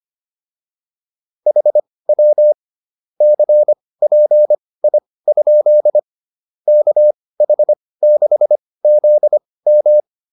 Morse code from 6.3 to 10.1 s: KH6ZM

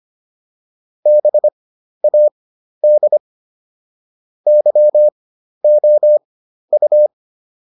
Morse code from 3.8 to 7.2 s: YOU

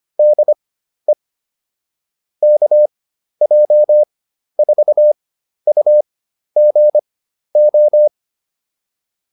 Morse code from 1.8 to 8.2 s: KJ4UGO

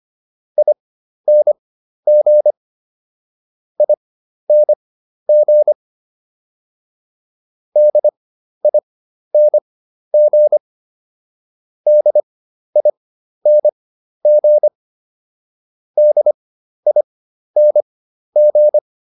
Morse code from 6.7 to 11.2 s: DING

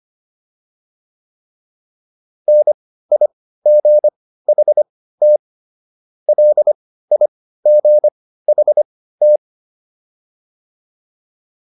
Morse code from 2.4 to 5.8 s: NIGHT

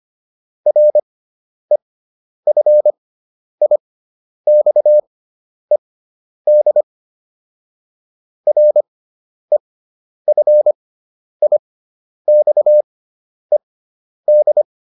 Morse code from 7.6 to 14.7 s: REFIXED

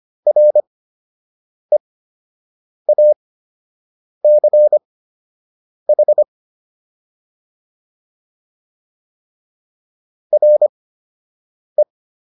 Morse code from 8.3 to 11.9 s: RE

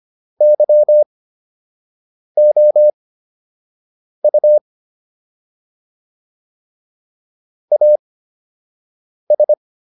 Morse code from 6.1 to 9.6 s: AS

